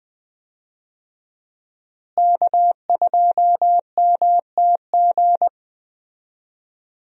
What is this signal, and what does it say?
Morse code, heard as a keyed tone, about 20 wpm, 700 Hz. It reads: K2MTG
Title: USA Call Signs - 5 Characters - Repeated 3x - 20wpm